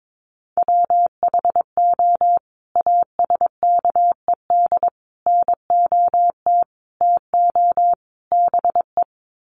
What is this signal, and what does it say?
Morse code, heard as a keyed tone, about 22 wpm, 700 Hz. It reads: WHO ASKED NOT TO BE
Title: Sets of 5 Words Encore2 22wpm